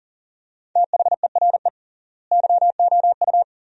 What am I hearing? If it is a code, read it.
Morse code, heard as a keyed tone, about 40 wpm, 700 Hz. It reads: THERE YOU
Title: 3x Word-Spacing - Sentences from Top 100 Words - 40wpm